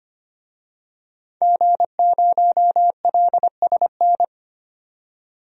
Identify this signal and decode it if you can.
Morse code, heard as a keyed tone, about 25 wpm, 700 Hz. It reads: G0LSN
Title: DXCC-100 Call Signs - 5 Characters - 25wpm